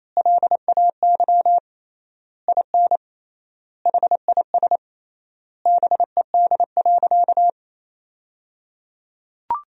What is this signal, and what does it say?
Morse code, heard as a keyed tone, about 28 wpm, 700 Hz. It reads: LAY IN HIS BED.
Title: 3x Word-Spacing - Sentences from Top 500 Words - 28wpm